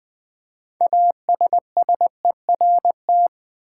Morse code, heard as a keyed tone, about 20 wpm, 700 Hz. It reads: ASSERT
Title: 6 Letter Words - First 3 Letters Given - 20wpm